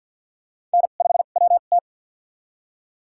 Morse code, heard as a keyed tone, about 52 wpm, spaced as for 36 wpm, 700 Hz. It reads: N5WT
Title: USA Call Signs - 4 Characters - 2x Character Spacing - 50wpm